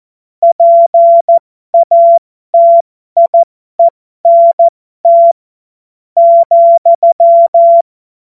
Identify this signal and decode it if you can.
Morse code, heard as a keyed tone, about 14 wpm, spaced as for 11 wpm, 680 Hz. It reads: PATIENT,
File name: Morsekurs / Witze